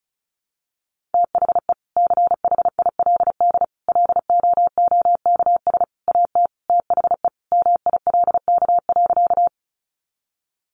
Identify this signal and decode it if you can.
Morse code, heard as a keyed tone, about 34 wpm, 700 Hz. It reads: THE CHILD LOOKS AT THE MILK.
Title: Additional Easy Phrases 35wpm